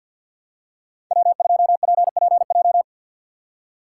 Morse code, heard as a keyed tone, about 52 wpm, 700 Hz. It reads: W2PPJ
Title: USA Call Signs - 4x Character Spacing - 50wpm